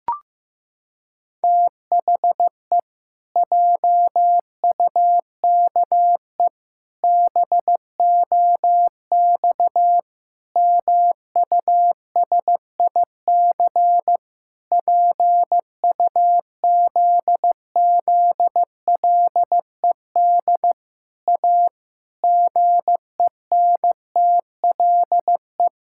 Morse code, heard as a keyed tone, about 15 wpm, 700 Hz. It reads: THE JUKE BOX MUSIC PUZZLED A GENTLE
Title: Pangram Challenge v2 - 15wpm